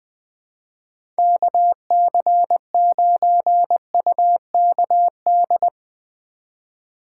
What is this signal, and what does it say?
Morse code, heard as a keyed tone, about 20 wpm, 700 Hz. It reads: KC9UKD
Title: USA Call Signs - 6 Characters - 2x Character Spacing - 20wpm